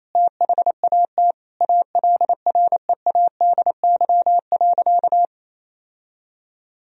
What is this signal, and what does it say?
Morse code, heard as a keyed tone, about 28 wpm, 700 Hz. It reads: THAT ALREADY.
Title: Sentences from Top 1000 Words - Review Part 3 - 28wpm